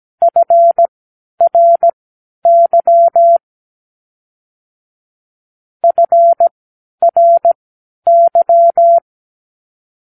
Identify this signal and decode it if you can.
Morse code, heard as a keyed tone, about 17 wpm, spaced as for 10 wpm, 680 Hz. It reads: FRY FRY
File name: k7qo / Practicewords